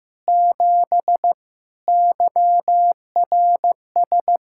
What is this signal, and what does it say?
Morse code, heard as a keyed tone, about 15 wpm, 700 Hz. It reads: 7 YRS
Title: Practice QSO - Exchange 3 - More Info - 15wpm